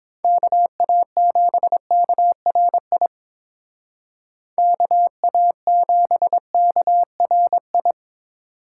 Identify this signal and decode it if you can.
Morse code, heard as a keyed tone, about 29 wpm, 700 Hz. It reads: KA7KRI KA7KRI